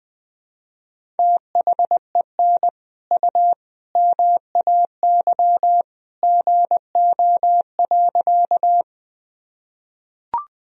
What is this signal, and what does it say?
Morse code, heard as a keyed tone, about 21 wpm, 700 Hz. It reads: THEN U MAY GO.
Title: Sentences from Top 100 Words and Abbreviations 20wpm